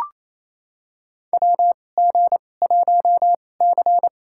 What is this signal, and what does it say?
Morse code, heard as a keyed tone, about 28 wpm, spaced as for 20 wpm, 700 Hz. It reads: WG1C
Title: USA Call Signs - 2x Character Spacing - 28wpm